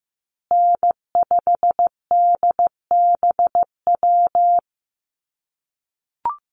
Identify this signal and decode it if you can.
Morse code, heard as a keyed tone, about 15 wpm, 700 Hz. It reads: N5DBW
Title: USA Call Signs Repeated 2x 15wpm